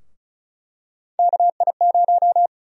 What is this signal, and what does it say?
Morse code, heard as a keyed tone, about 34 wpm, 700 Hz. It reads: KI0
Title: USA Call Sign Prefixes - Repeated 2x - 35wpm